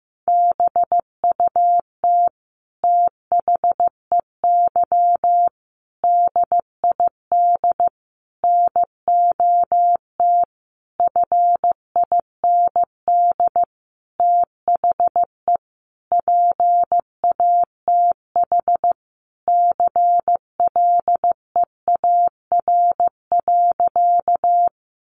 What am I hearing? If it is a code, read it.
Morse code, heard as a keyed tone, about 15 wpm, 700 Hz. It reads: BUT THEY DID NOT FIND THE PATH CLEAR.
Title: Sentences from Top 1400 Words 15wpm